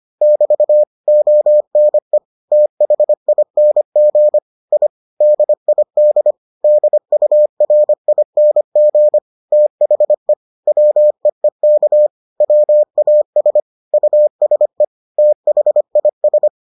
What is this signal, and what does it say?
Morse code, heard as a keyed tone, about 25 wpm, 600 Hz. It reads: <BT> ONE THING I DID DURING THE WEEK WAS USE THIS